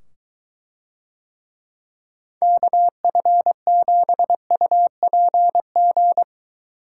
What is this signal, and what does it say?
Morse code, heard as a keyed tone, about 22 wpm, 700 Hz. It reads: KF7UPG